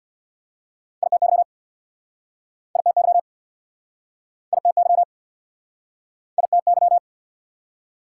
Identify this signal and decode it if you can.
Morse code, heard as a keyed tone, about 69 wpm, 690 Hz. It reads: ITY ITY ITY ITY